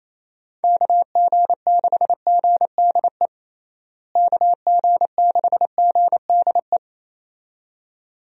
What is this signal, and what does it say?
Morse code, heard as a keyed tone, about 28 wpm, 700 Hz. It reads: KG6GDE KG6GDE
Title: USA Call Signs - 6 Characters - Repeated 2x - 28wpm